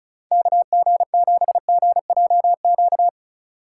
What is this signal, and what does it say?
Morse code, heard as a keyed tone, about 40 wpm, 700 Hz. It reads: KG7GJQ